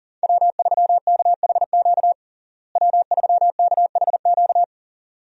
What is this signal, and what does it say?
Morse code, heard as a keyed tone, about 40 wpm, 700 Hz. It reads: W3KHQ W3KHQ
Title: USA Call Signs - 5 Characters - Repeated 2x - 40wpm